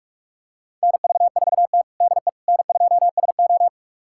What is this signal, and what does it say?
Morse code, heard as a keyed tone, about 41 wpm, 700 Hz. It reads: NV4T DE N2SO
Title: USA Callsign DE Callsign - 4 Characters - 45wpm